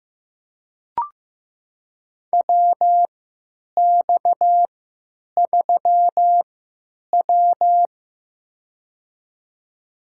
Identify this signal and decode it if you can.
Morse code, heard as a keyed tone, about 15 wpm, spaced as for 8 wpm, 700 Hz. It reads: WX3W
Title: USA Call Signs - 4 Characters - 3x Character Spacing - 15wpm